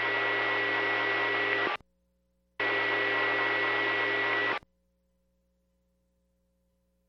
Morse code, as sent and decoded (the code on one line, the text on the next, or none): none